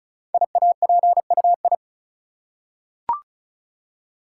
Morse code, heard as a keyed tone, about 36 wpm, 700 Hz. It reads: IAPUI
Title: Code Groups - 5 Letters - 35wpm